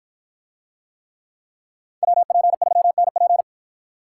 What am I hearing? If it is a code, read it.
Morse code, heard as a keyed tone, about 53 wpm, 700 Hz. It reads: WP3NP